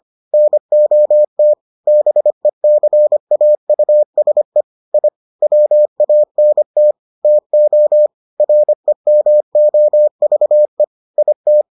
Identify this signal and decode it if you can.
Morse code, heard as a keyed tone, about 25 wpm, 600 Hz. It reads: NOT BECAUSE I WANT TO REMOVE IT